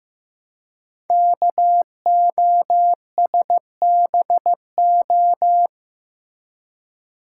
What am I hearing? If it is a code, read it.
Morse code, heard as a keyed tone, about 15 wpm, 700 Hz. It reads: KOSBO
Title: Code Groups - 5 Letters - 15wpm